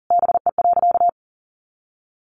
Morse code, heard as a keyed tone, about 40 wpm, 700 Hz. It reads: BE.
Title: Sentences from Top 200 Words 40wpm